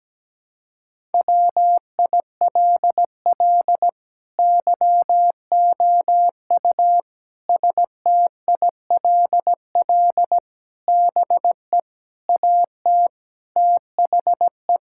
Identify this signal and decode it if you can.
Morse code, heard as a keyed tone, about 17 wpm, 700 Hz. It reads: WILL YOU STILL BE AT THE
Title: Sentences from Top 1000 Words - Review Part 5 - 17wpm